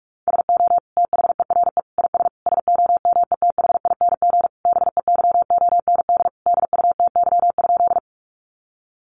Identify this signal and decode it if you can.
Morse code, heard as a keyed tone, about 44 wpm, 700 Hz. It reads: SO THERE IS SOMETHING BEYOND DUTY?
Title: Sentences from Top 1500 Words 45wpm